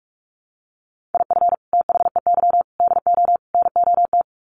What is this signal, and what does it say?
Morse code, heard as a keyed tone, about 44 wpm, 700 Hz. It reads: IF THEY DO NOT
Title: Sets of 4 Words 45wpm